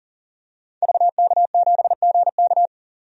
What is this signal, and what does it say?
Morse code, heard as a keyed tone, about 40 wpm, 700 Hz. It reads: VK7GK
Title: DXCC-100 Call Signs - 5 Characters - 40wpm